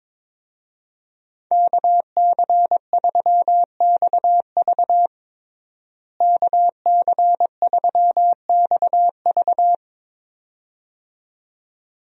Morse code, heard as a keyed tone, about 22 wpm, 700 Hz. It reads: KC3XV KC3XV
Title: USA Call Signs - 5 Characters - Repeated 2x - 22wpm